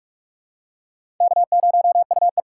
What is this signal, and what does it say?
Morse code, heard as a keyed tone, about 44 wpm, 700 Hz. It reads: K0UE
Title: USA Call Signs - 4 Characters - 2x Character Spacing - 45wpm